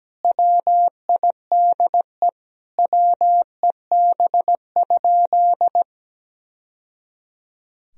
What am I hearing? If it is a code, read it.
Morse code, heard as a keyed tone, about 17 wpm, 700 Hz. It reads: WIDE WEB?
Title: Sentences from Top 1000 Words - Review Part 4 - 17wpm